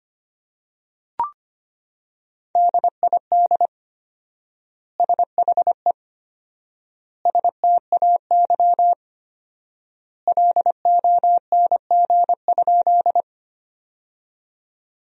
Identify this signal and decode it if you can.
Morse code, heard as a keyed tone, about 25 wpm, 700 Hz. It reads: DID SHE STAY LONG?
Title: 4x Word-Spacing - Sentences from Top 500 Words - 25wpm